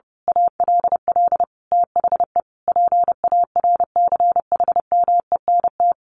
Morse code, heard as a keyed tone, about 30 wpm, 700 Hz. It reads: ALL THE PARCHMENT
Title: A Sound Of Thunder 30wpm